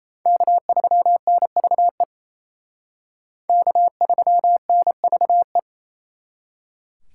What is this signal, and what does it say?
Morse code, heard as a keyed tone, about 34 wpm, 700 Hz. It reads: K3NVE K3NVE
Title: Speed Racing - USA 1x3 Call Signs 28wpm